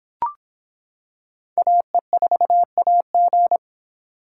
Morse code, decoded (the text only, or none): AE4AG